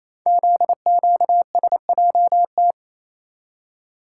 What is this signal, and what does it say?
Morse code, heard as a keyed tone, about 27 wpm, 700 Hz. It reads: ZQSJT
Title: Code Groups - 5 Letters - 28wpm